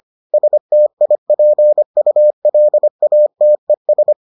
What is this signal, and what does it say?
Morse code, heard as a keyed tone, about 25 wpm, 600 Hz. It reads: STIPULATES